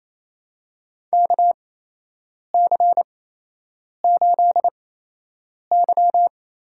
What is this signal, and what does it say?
Morse code, heard as a keyed tone, about 28 wpm, spaced as for 8 wpm, 700 Hz. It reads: KC8Y
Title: USA Call Signs - 8x Character Spacing - 28wpm